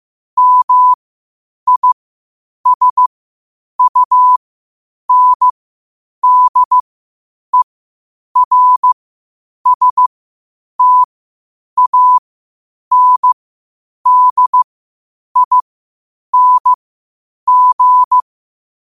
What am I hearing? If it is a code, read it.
Morse code, heard as a keyed tone, about 15 wpm, spaced as for 8 wpm, 1000 Hz. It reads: MISUNDERSTANDING
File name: qotdfiles / 8wpm